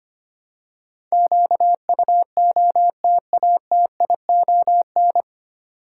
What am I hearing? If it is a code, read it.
Morse code, heard as a keyed tone, about 25 wpm, 700 Hz. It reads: QUOTATION
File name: Nine Letter Words - 25wpm